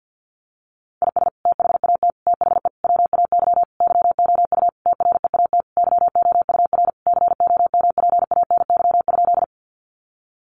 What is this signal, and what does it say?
Morse code, heard as a keyed tone, about 50 wpm, 700 Hz. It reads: IS THAT THE WAY YOU TREAT YOUR COMPANY?
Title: Sentences from Top 1300 Words 50wpm